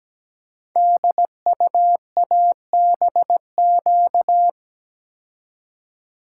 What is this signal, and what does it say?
Morse code, heard as a keyed tone, about 17 wpm, 700 Hz. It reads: DUABQ
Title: Code Groups - 5 Letters - 17wpm